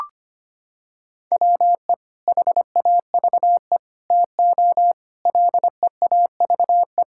Morse code, heard as a keyed tone, about 25 wpm, 700 Hz. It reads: WE HAVE TO LEAVE
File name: practice / 25wpm / Sentences with 3-Word Phrases - 25wpm